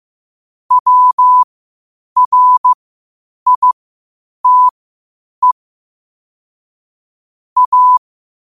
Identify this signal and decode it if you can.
Morse code, heard as a keyed tone, about 15 wpm, spaced as for 8 wpm, 1000 Hz. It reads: WRITE A
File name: qotdfiles / 8wpm